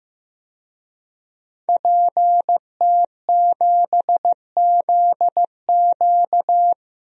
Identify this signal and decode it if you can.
Morse code, heard as a keyed tone, about 15 wpm, 700 Hz. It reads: PT7ZQ
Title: DXCC-100 Call Signs - 5 Characters - 15wpm